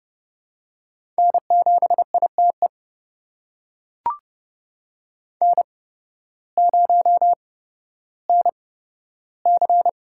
Morse code, heard as a keyed tone, about 30 wpm, spaced as for 14 wpm, 700 Hz. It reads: N7ITE N0NC
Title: USA Call Signs - 5 Characters - 8x Character Spacing - 30wpm